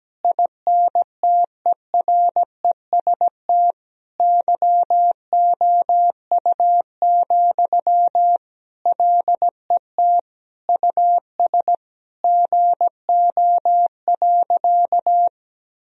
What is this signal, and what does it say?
Morse code, heard as a keyed tone, about 17 wpm, 700 Hz. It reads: INTEREST YOU, LET US GO.
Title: Sentences from Top 1000 Words - Review Part 2 - 17wpm